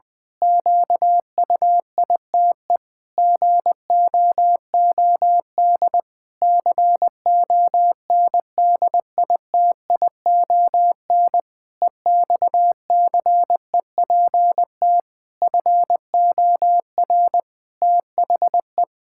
Morse code, heard as a keyed tone, about 20 wpm, 700 Hz. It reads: QUITE GOOD CONDITION EXCEPT FOR THE
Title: Pangram Challenge v2 - 20wpm